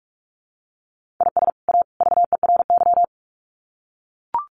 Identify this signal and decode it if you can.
Morse code, heard as a keyed tone, about 44 wpm, 700 Hz. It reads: IS A VERY
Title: Sets of 3 Words 45wpm